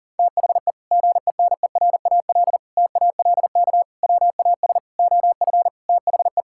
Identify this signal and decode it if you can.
Morse code, heard as a keyed tone, about 40 wpm, 700 Hz. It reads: THE GENERAL TALK WAS OF THE
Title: Sentences from Top 1000 Words - Review Part 5 - 40wpm